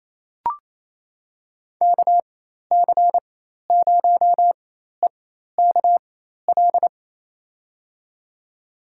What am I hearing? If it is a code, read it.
Morse code, heard as a keyed tone, about 27 wpm, spaced as for 13 wpm, 700 Hz. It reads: KC0EKL
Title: USA Call Signs - 6 Characters - 4x Character Spacing - 28wpm